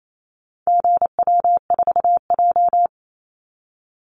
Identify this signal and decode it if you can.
Morse code, heard as a keyed tone, about 28 wpm, 700 Hz. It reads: GW4J
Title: Call Signs Repeated 2x - 28wpm